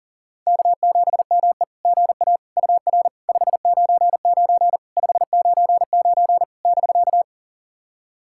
Morse code, heard as a keyed tone, about 40 wpm, 690 Hz. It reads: K7ME GA UR 599 599 <BK>
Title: SOTA QSO Practice 40wpm